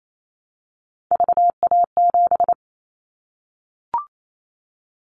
Morse code, heard as a keyed tone, about 28 wpm, 700 Hz. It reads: VA7
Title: Callsign Prefixes for Top 25 DXCC - 28wpm